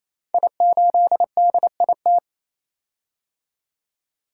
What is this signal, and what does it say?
Morse code, heard as a keyed tone, about 27 wpm, 700 Hz. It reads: I8DIT
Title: Code Groups - 5 Letter:Numbers - 28wpm